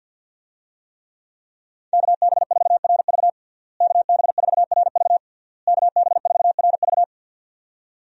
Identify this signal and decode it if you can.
Morse code, heard as a keyed tone, about 52 wpm, 700 Hz. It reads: KB4RV KB4RV KB4RV
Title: USA Call Signs - 5 Characters - Repeated 3x - 50wpm